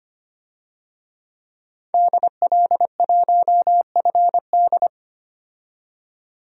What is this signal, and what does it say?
Morse code, heard as a keyed tone, about 25 wpm, 700 Hz. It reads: DL1FD
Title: DXCC-100 Call Signs - 5 Characters - 25wpm